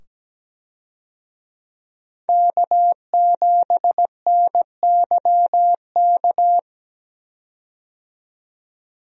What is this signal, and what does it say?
Morse code, heard as a keyed tone, about 17 wpm, 700 Hz. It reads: K7NYK
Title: USA Call Signs - 5 Characters - 17wpm